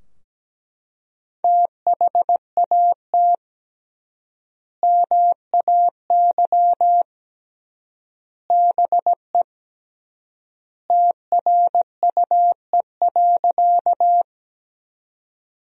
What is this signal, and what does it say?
Morse code, heard as a keyed tone, about 17 wpm, 700 Hz. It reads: THAT MAY BE TRUE.
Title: 3x Word-Spacing - Sentences from Top 400 Words - 17wpm